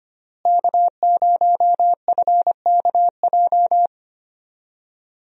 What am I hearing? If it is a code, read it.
Morse code, heard as a keyed tone, about 25 wpm, 690 Hz. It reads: K0FKJ
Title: USA Call Signs - 5 Characters - Repeated 2x - 25wpm